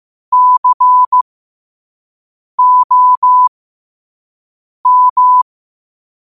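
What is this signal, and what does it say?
Morse code, heard as a keyed tone, about 15 wpm, spaced as for 5 wpm, 1000 Hz. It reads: COM
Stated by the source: 5 wpm Morse code practice recording.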